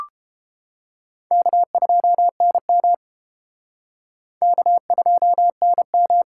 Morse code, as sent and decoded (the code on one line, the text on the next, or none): -.- ..--- -. -- / -.- ..--- -. --
K2NM K2NM